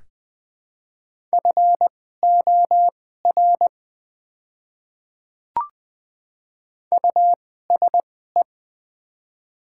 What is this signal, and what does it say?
Morse code, heard as a keyed tone, about 21 wpm, spaced as for 15 wpm, 700 Hz. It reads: FOR USE
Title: 2x Character Spacing - Top 200 Words - 20wpm